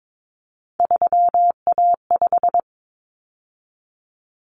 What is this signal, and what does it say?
Morse code, heard as a keyed tone, about 22 wpm, 700 Hz. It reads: 3A5
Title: Common Cut Numbers - 3-Digit 22wpm